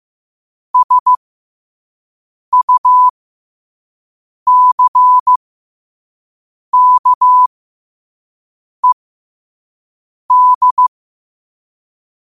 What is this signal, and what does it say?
Morse code, heard as a keyed tone, about 15 wpm, spaced as for 5 wpm, 1000 Hz. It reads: SUCKED
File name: qotdfiles / 5wpm